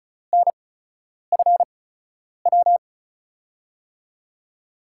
Morse code, heard as a keyed tone, about 36 wpm, spaced as for 10 wpm, 700 Hz. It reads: NFW
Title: USA Call Signs - 5 Characters - 8x Character Spacing - 35wpm